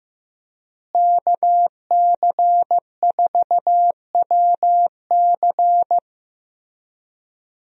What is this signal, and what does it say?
Morse code, heard as a keyed tone, about 15 wpm, 700 Hz. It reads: KC4WC
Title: USA Call Signs - 5 Characters - Repeated 3x - 15wpm